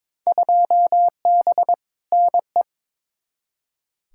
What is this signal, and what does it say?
Morse code, heard as a keyed tone, about 22 wpm, 700 Hz. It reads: 2B NE
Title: Field Day Exchanges Encore - 22wpm